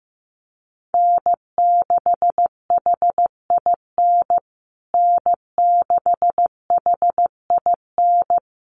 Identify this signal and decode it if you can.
Morse code, heard as a keyed tone, about 15 wpm, 700 Hz. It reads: N6HIN N6HIN
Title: USA Call Signs Repeated 2x 15wpm